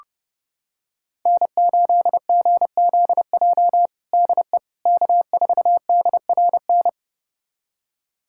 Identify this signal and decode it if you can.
Morse code, heard as a keyed tone, about 32 wpm, 700 Hz. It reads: N8GZJ DE K4DRN
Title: USA Callsign DE Callsign 30wpm